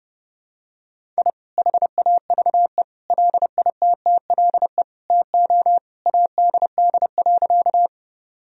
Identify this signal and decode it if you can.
Morse code, heard as a keyed tone, about 30 wpm, 700 Hz. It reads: I HAVE LITTLE TO ADD.
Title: Sentences from Top 1000 Words - Review Part 1 - 30wpm